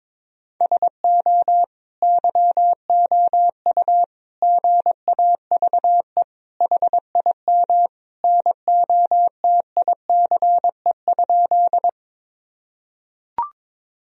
Morse code, heard as a keyed tone, about 22 wpm, 700 Hz. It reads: SO YOU GAVE HIM NOTICE?
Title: Sentences from Top 1000 Words - Review Part 2 - 22wpm